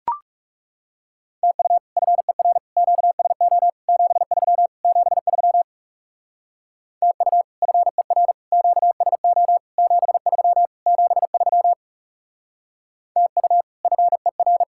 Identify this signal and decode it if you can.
Morse code, heard as a keyed tone, about 38 wpm, 700 Hz. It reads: TU FER QSO 73 73 TU FER QSO 73 73 TU FER